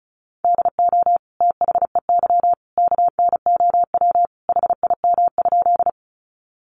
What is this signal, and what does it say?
Morse code, heard as a keyed tone, about 34 wpm, 700 Hz. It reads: DO THEY KNOW HIM?
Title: Sentences from Top 100 Words 35wpm